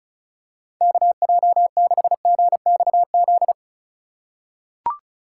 Morse code, heard as a keyed tone, about 34 wpm, 700 Hz. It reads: KJ6GXZ
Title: USA Call Signs - 6 Characters - 3x Character Spacing - 35wpm